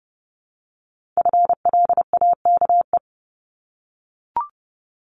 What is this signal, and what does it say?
Morse code, heard as a keyed tone, about 30 wpm, 700 Hz. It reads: FLAKE
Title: Multi-letter Phonemes 30wpm